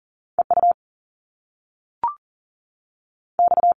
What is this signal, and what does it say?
Morse code, heard as a keyed tone, about 42 wpm, 700 Hz. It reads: E U X
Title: Top Two Letter Combinations in Words 40wpm